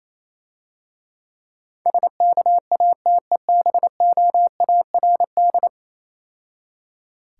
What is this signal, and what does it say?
Morse code, heard as a keyed tone, about 27 wpm, 700 Hz. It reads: SKATEBOARD
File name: Easy Words - Speed-Racing with Hints - 22wpm